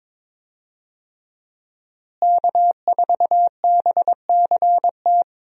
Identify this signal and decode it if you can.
Morse code, heard as a keyed tone, about 22 wpm, 700 Hz. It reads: K4BCT
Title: USA Call Signs - 5 Characters - Repeated 3x - 22wpm